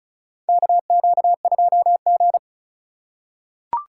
Morse code, heard as a keyed tone, about 36 wpm, 700 Hz. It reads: KQ2G
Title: USA Call Signs - 4 Characters - 4x Character Spacing - 35wpm